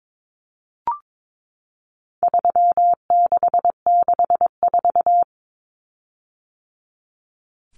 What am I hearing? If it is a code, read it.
Morse code, heard as a keyed tone, about 22 wpm, 700 Hz. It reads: 3664
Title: Common Cut Numbers - 4-Digit 22wpm